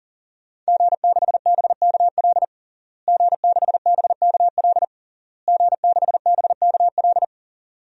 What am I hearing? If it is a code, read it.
Morse code, heard as a keyed tone, about 40 wpm, 700 Hz. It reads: G6BKL G6BKL G6BKL
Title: DXCC-100 Call Signs - 5 Characters - Repeated 3x - 40wpm